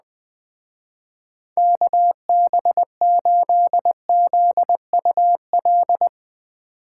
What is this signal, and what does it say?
Morse code, heard as a keyed tone, about 20 wpm, 700 Hz. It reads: KB8ZUL